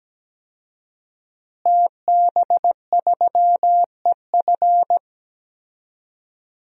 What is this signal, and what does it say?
Morse code, heard as a keyed tone, about 17 wpm, 700 Hz. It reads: TB3EF
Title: DXCC-100 Call Signs - 5 Characters - Repeated 3x - 17wpm